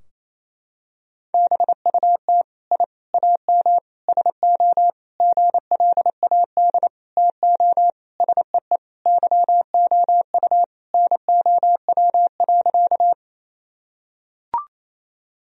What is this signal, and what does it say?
Morse code, heard as a keyed tone, about 27 wpm, 700 Hz. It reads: BUT I AM SO GLAD TO SEE YOU NOW.
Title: Sentences from Top 1000 Words - Review Part 2 - 28wpm